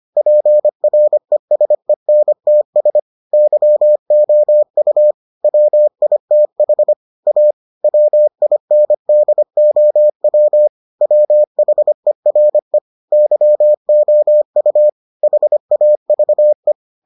Morse code, heard as a keyed tone, about 25 wpm, 600 Hz. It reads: PRESENTS YOU WITH A WINDOW WHERE YOU HAVE